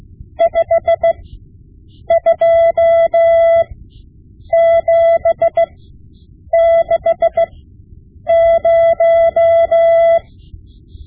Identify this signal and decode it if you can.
Morse code, heard as a keyed tone, about 16 wpm, spaced as for 7 wpm, 670 Hz. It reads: 52760
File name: Morse Lessons / Morse Lessons / Tape-5B